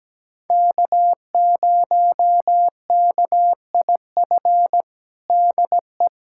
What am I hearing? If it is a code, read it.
Morse code, heard as a keyed tone, about 17 wpm, 700 Hz. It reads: K0KIF DE
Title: USA Callsign DE Callsign - 5 Characters - 17wpm